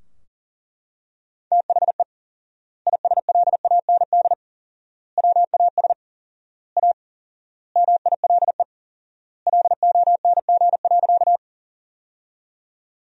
Morse code, heard as a keyed tone, about 37 wpm, 700 Hz. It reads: THE ISLAND WAS A MILE LONG.
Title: 4x Word-Spacing - Sentences from Top 500 Words - 40wpm